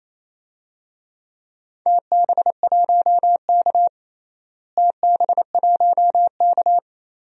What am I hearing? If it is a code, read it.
Morse code, heard as a keyed tone, about 28 wpm, 700 Hz. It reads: TB1K TB1K
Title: DXCC-100 Call Signs - 4 Characters - Repeated 2x - 28wpm